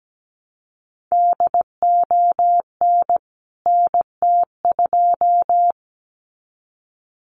Morse code, heard as a keyed tone, about 17 wpm, 700 Hz. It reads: DON NT2
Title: CWT Exchange 17wpm